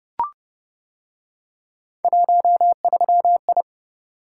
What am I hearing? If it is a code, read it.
Morse code, heard as a keyed tone, about 28 wpm, 700 Hz. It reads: 13I